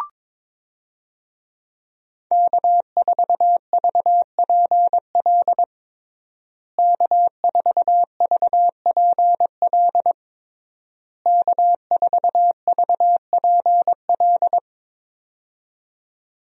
Morse code, heard as a keyed tone, about 22 wpm, 700 Hz. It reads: K4VPL K4VPL K4VPL
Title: USA Call Signs - 5 Characters - Repeated 3x - 22wpm